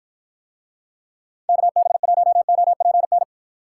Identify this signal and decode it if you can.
Morse code, heard as a keyed tone, about 53 wpm, 700 Hz. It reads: KB1OPN